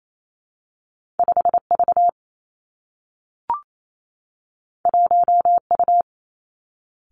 Morse code, heard as a keyed tone, about 28 wpm, 700 Hz. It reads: 5 V 1 U
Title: Two Challenging Letter-Number Combinations 28wpm